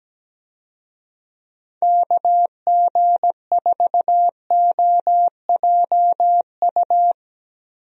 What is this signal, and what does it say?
Morse code, heard as a keyed tone, about 17 wpm, 700 Hz. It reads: KG4OJU
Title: USA Call Signs - 6 Characters - 17wpm